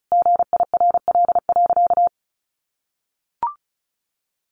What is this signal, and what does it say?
Morse code, heard as a keyed tone, about 34 wpm, 700 Hz. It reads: GIRL.
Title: Sentences from Top 400 Words 35wpm